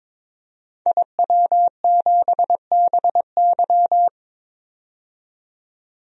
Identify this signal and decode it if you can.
Morse code, heard as a keyed tone, about 22 wpm, 700 Hz. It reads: IW7BY
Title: DXCC-100 Call Signs - 5 Characters - 22wpm